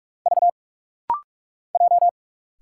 Morse code, heard as a keyed tone, about 45 wpm, 700 Hz. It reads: V J
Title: Single Letters - Rapid-Fire 45wpm